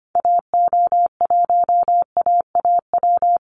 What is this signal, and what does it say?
Morse code, heard as a keyed tone, about 25 wpm, 700 Hz. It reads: AO1AAW
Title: Call Signs - 25wpm